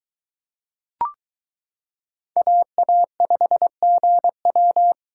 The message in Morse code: AA5GW

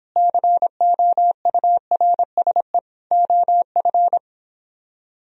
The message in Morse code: COURSE OF

